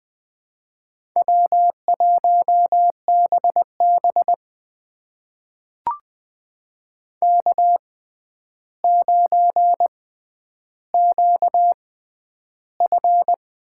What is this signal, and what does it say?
Morse code, heard as a keyed tone, about 20 wpm, spaced as for 10 wpm, 700 Hz. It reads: W1BB K9QF
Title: USA Call Signs - 4 Characters - 6x Character Spacing - 20wpm